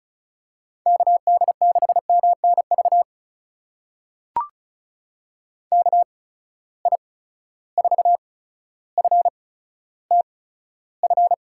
Morse code, heard as a keyed tone, about 36 wpm, spaced as for 15 wpm, 700 Hz. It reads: KD6MNV KI4FTF
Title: USA Call Signs - 6 Characters - 8x Character Spacing - 35wpm